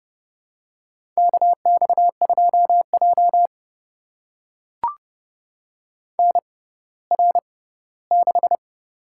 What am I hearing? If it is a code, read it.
Morse code, heard as a keyed tone, about 30 wpm, spaced as for 17 wpm, 690 Hz. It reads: KX2J NR6
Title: USA Call Signs - 4 Characters - 6x Character Spacing - 30wpm